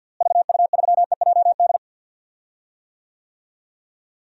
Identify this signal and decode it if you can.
Morse code, heard as a keyed tone, about 52 wpm, 700 Hz. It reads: VU3EJD
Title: DXCC-100 Call Signs - 6 Characters - Repeated 3x - 50wpm